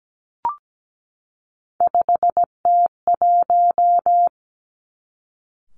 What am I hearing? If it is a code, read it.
Morse code, heard as a keyed tone, about 17 wpm, 700 Hz. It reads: 5T1
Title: Three Challenging Letter-Number Combinations 17wpm